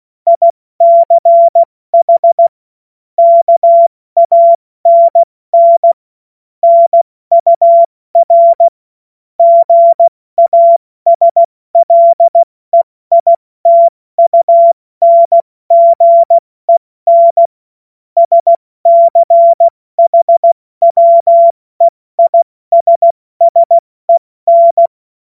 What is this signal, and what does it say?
Morse code, heard as a keyed tone, about 16 wpm, 680 Hz. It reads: ICH KANN NUR GASLEITUNGEN SCHWEISSEN
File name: Morsekurs / Witze / Witz 31.Oktober 2023 70 F80